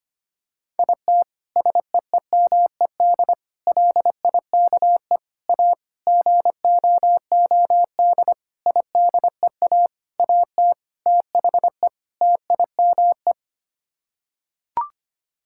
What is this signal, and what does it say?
Morse code, heard as a keyed tone, about 25 wpm, 700 Hz. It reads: IT SEEMED LIKE A GOOD IDEA AT THE TIME
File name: Sets of 9 Words - 25wpm